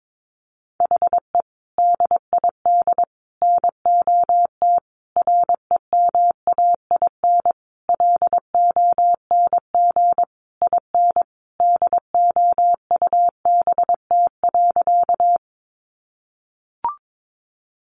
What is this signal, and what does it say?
Morse code, heard as a keyed tone, about 22 wpm, 700 Hz. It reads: HE DID NOT REMAIN LONG IN DOUBT.
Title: Sentences from Top 1500 Words 22wpm